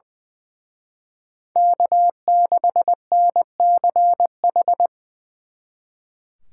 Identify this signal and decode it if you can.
Morse code, heard as a keyed tone, about 20 wpm, 700 Hz. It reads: K6NCH
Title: Speed Racing - USA Call Signs 20wpm